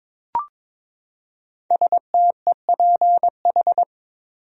Morse code, heard as a keyed tone, about 22 wpm, 700 Hz. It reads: STEPH
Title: USA Common Names - 22wpm